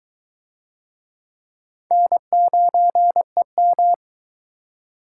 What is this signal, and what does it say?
Morse code, heard as a keyed tone, about 21 wpm, 700 Hz. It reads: N9EM